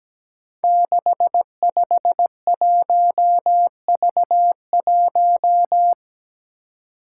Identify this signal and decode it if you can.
Morse code, heard as a keyed tone, about 17 wpm, 700 Hz. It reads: 651V1
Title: Five Challenging Letter-Number Combinations 17wpm